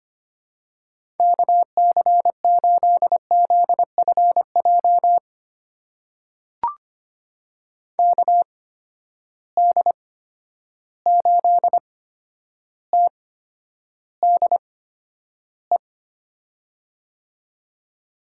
Morse code, heard as a keyed tone, about 25 wpm, spaced as for 11 wpm, 700 Hz. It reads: KC8ZFJ KD8TDE
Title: USA Call Signs - 6 Characters - 8x Character Spacing - 25wpm